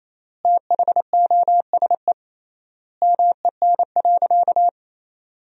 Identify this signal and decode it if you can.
Morse code, heard as a keyed tone, about 28 wpm, 700 Hz. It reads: THOSE MEN.
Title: 3x Word-Spacing - Sentences from Top 300 Words - 28wpm